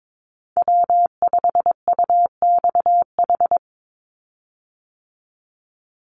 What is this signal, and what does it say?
Morse code, heard as a keyed tone, about 22 wpm, 700 Hz. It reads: W5UXH
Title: Call Signs Repeated 2x - 22wpm